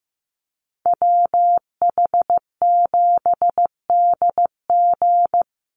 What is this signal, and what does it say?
Morse code, heard as a keyed tone, about 15 wpm, 700 Hz. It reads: WH7DG
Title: USA Call Signs 15wpm